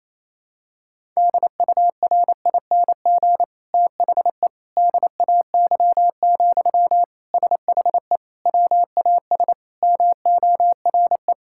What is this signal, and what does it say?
Morse code, heard as a keyed tone, about 28 wpm, 700 Hz. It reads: DURING THE DAY, SHE WAS MORE
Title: Sentences from Top 1000 Words - Review Part 6 - 28wpm